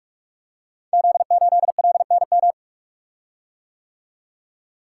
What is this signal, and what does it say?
Morse code, heard as a keyed tone, about 44 wpm, 700 Hz. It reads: Z8LNM
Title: Code Groups - 5 Letter:Numbers - 45wpm